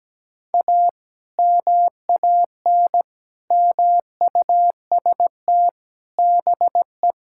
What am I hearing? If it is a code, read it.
Morse code, heard as a keyed tone, about 17 wpm, 700 Hz. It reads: A MAN MUST BE
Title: Sentences from Top 1000 Words - Review Part 6 - 17wpm